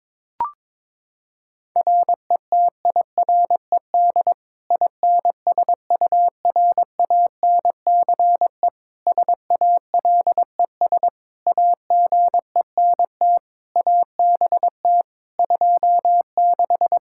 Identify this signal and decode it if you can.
Morse code, heard as a keyed tone, about 22 wpm, 700 Hz. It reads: RETIRED INSURANCE SALES AGENT ABT 26
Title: Practice QSO - Exchange 3 - More Info - 22wpm